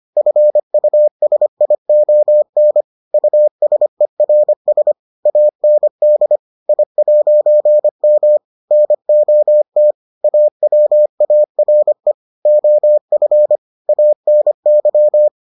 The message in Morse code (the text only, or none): FUSION USERS AND I'M NOT AWARE OF ANY